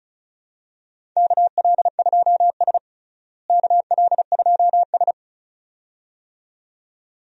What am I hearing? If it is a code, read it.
Morse code, heard as a keyed tone, about 36 wpm, 700 Hz. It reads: KL2S KL2S
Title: USA Call Signs - 4 Characters - Repeated 2x - 35wpm